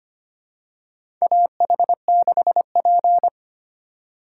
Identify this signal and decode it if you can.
Morse code, heard as a keyed tone, about 25 wpm, 700 Hz. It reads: AH6P
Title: USA Call Signs - 4 Characters - 2x Character Spacing - 25wpm